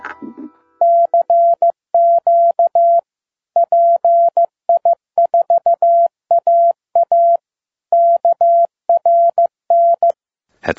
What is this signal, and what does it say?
Morse code, heard as a keyed tone, about 15 wpm, 680 Hz. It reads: CQ PI4AA KRN
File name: pi4aa / PI4AA Uitzending